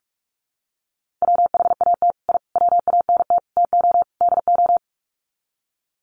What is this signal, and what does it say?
Morse code, heard as a keyed tone, about 44 wpm, 700 Hz. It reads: WHAT I WANT TO DO
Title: Sets of 5 Words Encore 45wpm